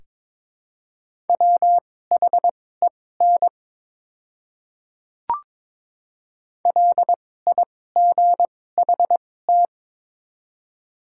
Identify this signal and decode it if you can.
Morse code, heard as a keyed tone, about 22 wpm, spaced as for 16 wpm, 700 Hz. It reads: WHEN LIGHT
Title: 2x Character Spacing - Top 300 Words - 22wpm